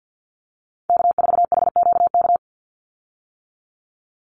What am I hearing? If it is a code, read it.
Morse code, heard as a keyed tone, about 50 wpm, 700 Hz. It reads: K4HQK
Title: Call Signs - 50wpm